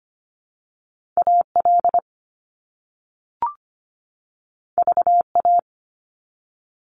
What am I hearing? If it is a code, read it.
Morse code, heard as a keyed tone, about 25 wpm, 700 Hz. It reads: AL VA
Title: US States 25wpm